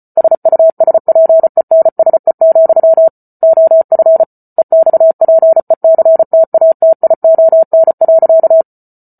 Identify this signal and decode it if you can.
Morse code, heard as a keyed tone, about 34 wpm, 650 Hz. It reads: SUSPENSE, OF EXPECTATION.